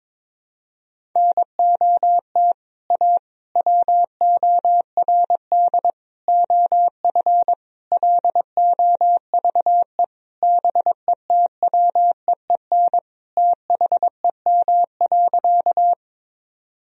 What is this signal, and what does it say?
Morse code, heard as a keyed tone, about 22 wpm, 700 Hz. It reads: NOT A WORD OF LOVE BETWEEN THEM.
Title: Sentences from Top 1000 Words - Review Part 1 - 22wpm